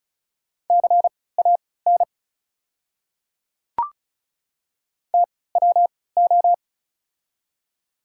Morse code, heard as a keyed tone, about 36 wpm, spaced as for 20 wpm, 690 Hz. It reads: CAN TWO
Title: 3x Character Spacing - Top 100 Words - 35wpm